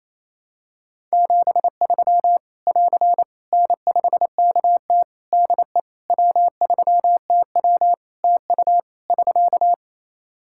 Morse code, heard as a keyed tone, about 28 wpm, 700 Hz. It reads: 73 <AR> N5KT DE W3TW TU <SK>
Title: Practice QSO - Exchange 4 - Ending - 28wpm